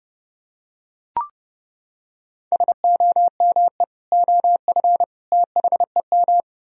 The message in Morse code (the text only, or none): SOME OF THEM